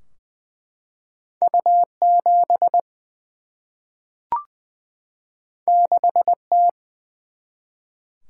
Morse code, heard as a keyed tone, about 20 wpm, 700 Hz. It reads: U 7 6 T
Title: Single-Letter Single-Number 20wpm